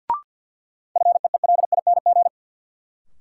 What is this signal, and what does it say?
Morse code, heard as a keyed tone, about 50 wpm, 700 Hz. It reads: FEELING
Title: Sentences from Top 1200 Words - New Word Drill Alt - 50wpm